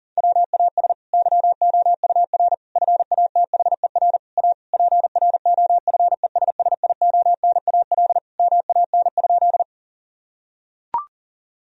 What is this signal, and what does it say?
Morse code, heard as a keyed tone, about 37 wpm, 700 Hz. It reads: WAS YOUR FATHER A PROFESSIONAL MAN?
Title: Sentences from Top 1000 Words - Review Part 4 - 40wpm